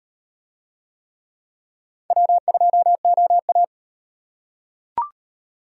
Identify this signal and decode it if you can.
Morse code, heard as a keyed tone, about 38 wpm, 700 Hz. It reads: W2OA